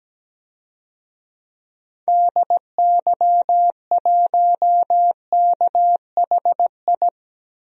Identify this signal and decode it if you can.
Morse code, heard as a keyed tone, about 17 wpm, 700 Hz. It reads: DY1KHI
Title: DXCC-100 Call Signs - 6 Characters - Repeated 2x - 17wpm